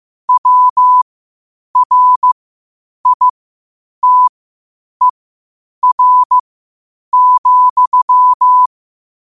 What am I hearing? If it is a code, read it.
Morse code, heard as a keyed tone, about 15 wpm, spaced as for 8 wpm, 1000 Hz. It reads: WRITER,
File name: qotdfiles / 8wpm